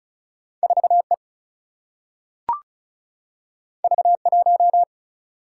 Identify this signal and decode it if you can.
Morse code, heard as a keyed tone, about 36 wpm, 700 Hz. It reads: 4E V1
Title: Single-Letter Single-Number 35wpm